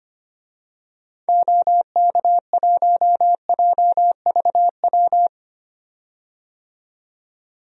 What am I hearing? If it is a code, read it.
Morse code, heard as a keyed tone, about 25 wpm, 700 Hz. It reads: OK1JVW
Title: DXCC-100 Call Signs - 6 Characters - 25wpm